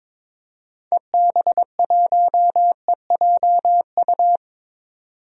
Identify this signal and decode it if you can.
Morse code, heard as a keyed tone, about 22 wpm, 700 Hz. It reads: EB1EJU
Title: DXCC-100 Call Signs - 6 Characters - 22wpm